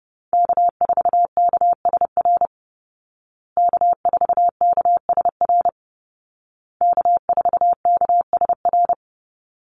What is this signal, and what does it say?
Morse code, heard as a keyed tone, about 30 wpm, 700 Hz. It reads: K4KSR K4KSR K4KSR
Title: Call Signs Repeated 2x - 30wpm